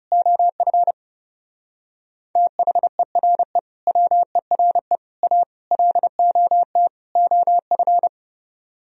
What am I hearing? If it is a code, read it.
Morse code, heard as a keyed tone, about 37 wpm, 700 Hz. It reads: OF THERE WERE A LOT OF
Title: Speed-Racing - Sets of 5 Words 30wpm